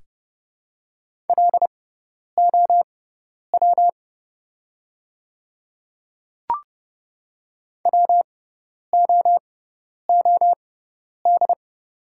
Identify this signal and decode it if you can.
Morse code, heard as a keyed tone, about 30 wpm, spaced as for 10 wpm, 700 Hz. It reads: LOW WOOD
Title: 6x Character Spacing - Top 300 Words - 30wpm